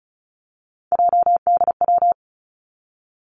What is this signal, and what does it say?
Morse code, heard as a keyed tone, about 34 wpm, 700 Hz. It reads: JDW
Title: Three Challenging Letter-Number Combinations 35wpm